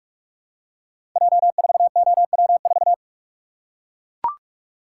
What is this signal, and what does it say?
Morse code, heard as a keyed tone, about 44 wpm, 700 Hz. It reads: J4OW4
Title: Code Groups - 5 Letter:Numbers - 45wpm